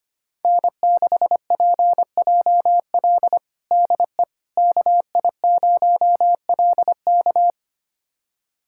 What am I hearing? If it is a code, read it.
Morse code, heard as a keyed tone, about 25 wpm, 700 Hz. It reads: N6PJL DE KI0LK
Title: USA Callsign DE Callsign - 5 Characters - 25wpm